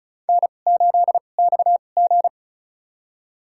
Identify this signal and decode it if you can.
Morse code, heard as a keyed tone, about 36 wpm, spaced as for 25 wpm, 700 Hz. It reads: N8XG
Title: USA Call Signs - 4 Characters - 2x Character Spacing - 35wpm